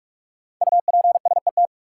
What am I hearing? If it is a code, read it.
Morse code, heard as a keyed tone, about 46 wpm, 700 Hz. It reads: UPSET
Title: Sentences from Top 800 Words - New Word Drill - 45wpm